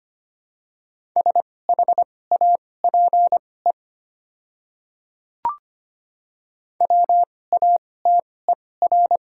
Morse code, heard as a keyed tone, about 25 wpm, spaced as for 18 wpm, 700 Hz. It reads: SHAPE WATER
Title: 2x Character Spacing - Top 500 Words - 25wpm